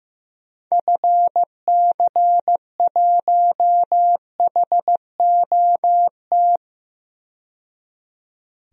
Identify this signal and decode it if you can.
Morse code, heard as a keyed tone, about 16 wpm, 700 Hz. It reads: FC1HOT
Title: DXCC-100 Call Signs - 6 Characters - 15wpm